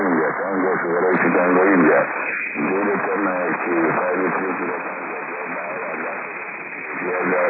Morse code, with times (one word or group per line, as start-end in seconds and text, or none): none